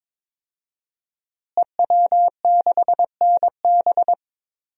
Morse code, heard as a keyed tone, about 22 wpm, 700 Hz. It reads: EW6NB
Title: DXCC-100 Call Signs - 5 Characters - Repeated 3x - 22wpm